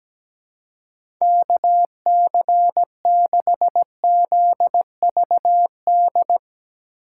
Morse code, heard as a keyed tone, about 17 wpm, 700 Hz. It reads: KC6ZVD
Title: USA Call Signs - 6 Characters - 2x Character Spacing - 17wpm